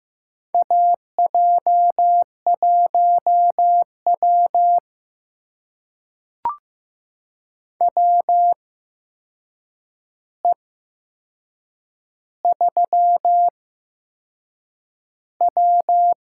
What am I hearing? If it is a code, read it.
Morse code, heard as a keyed tone, about 15 wpm, spaced as for 6 wpm, 700 Hz. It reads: AJ1W WE3W
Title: USA Call Signs - 4 Characters - 8x Character Spacing - 15wpm